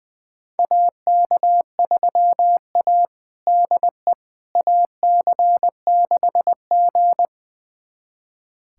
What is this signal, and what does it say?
Morse code, heard as a keyed tone, about 20 wpm, 700 Hz. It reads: AK3A DE AC6G
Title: USA Callsign DE Callsign - 4 Characters - 20wpm